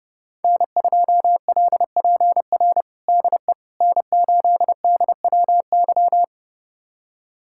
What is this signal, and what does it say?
Morse code, heard as a keyed tone, about 30 wpm, 700 Hz. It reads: N2LPR DE N8DWY
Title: USA Callsign DE Callsign - 5 Characters - 30wpm